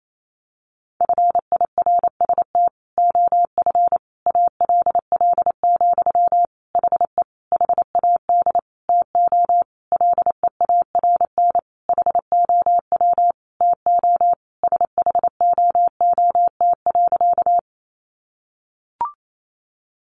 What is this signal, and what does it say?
Morse code, heard as a keyed tone, about 28 wpm, 700 Hz. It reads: FIRST OF ALL, HE HAD TO LEARN HOW TO SHOOT.
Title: Sentences from Top 1200 Words 28wpm